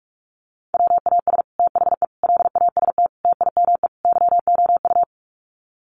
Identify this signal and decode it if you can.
Morse code, heard as a keyed tone, about 44 wpm, 700 Hz. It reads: WAS THE LAST TIME YOU
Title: Sets of 5 Words Encore2 45wpm